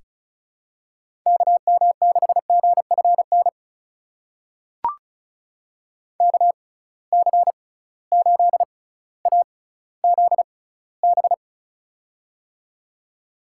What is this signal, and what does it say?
Morse code, heard as a keyed tone, about 36 wpm, spaced as for 18 wpm, 700 Hz. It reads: KM6GFN KC8AZB
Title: USA Call Signs - 6 Characters - 6x Character Spacing - 35wpm